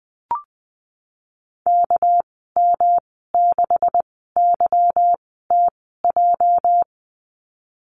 Morse code, heard as a keyed tone, about 19 wpm, spaced as for 14 wpm, 700 Hz. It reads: KM6YTJ
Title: 213 Lesson 34g - Number 8 - Call Signs 20wpm